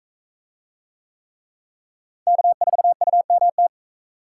Morse code, heard as a keyed tone, about 40 wpm, 700 Hz. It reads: K4UMT